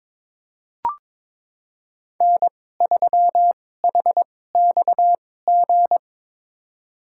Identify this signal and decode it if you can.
Morse code, heard as a keyed tone, about 22 wpm, spaced as for 16 wpm, 700 Hz. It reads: N3HXG
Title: USA Call Signs - 5 Characters - 2x Character Spacing - 22wpm